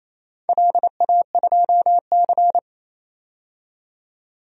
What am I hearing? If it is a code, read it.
Morse code, heard as a keyed tone, about 28 wpm, 700 Hz. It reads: LA2C
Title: DXCC-100 Call Signs - 4 Characters - 28wpm